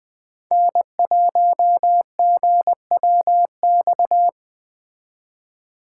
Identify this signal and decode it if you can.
Morse code, heard as a keyed tone, about 20 wpm, 700 Hz. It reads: N1GWX
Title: Speed Racing - USA 1x3 Call Signs 17wpm